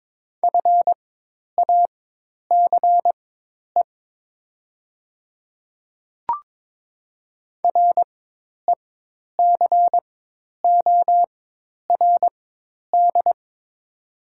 Morse code, heard as a keyed tone, about 22 wpm, spaced as for 10 wpm, 700 Hz. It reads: FACE RECORD
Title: 4x Character Spacing - Top 500 Words - 22wpm